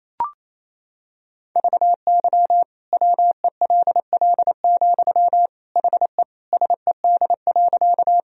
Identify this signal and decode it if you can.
Morse code, heard as a keyed tone, about 28 wpm, 700 Hz. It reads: VY WELL, HE SED.
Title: Sentences from Top 200 Words and Abbreviations 28wpm